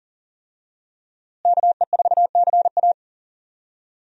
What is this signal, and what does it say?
Morse code, heard as a keyed tone, about 40 wpm, 700 Hz. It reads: KE4CA
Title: USA Call Signs - 5 Characters - Repeated 2x - 40wpm